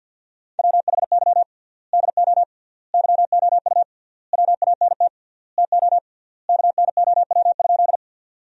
Morse code, heard as a keyed tone, about 49 wpm, 700 Hz. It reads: WHY DO YOU WANT TO KNOW?
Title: 3x Word-Spacing - Sentences from Top 200 Words - 50wpm